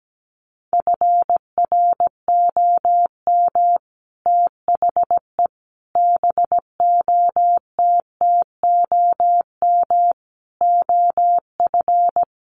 Morse code, heard as a keyed tone, about 17 wpm, 700 Hz. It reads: FROM THE BOTTOM OF
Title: Sets of 5 Words Encore2 17wpm